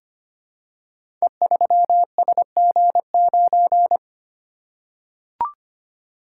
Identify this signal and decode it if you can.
Morse code, heard as a keyed tone, about 25 wpm, 700 Hz. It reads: E3SG9
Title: Code Groups - 5 Letter:Numbers - 25wpm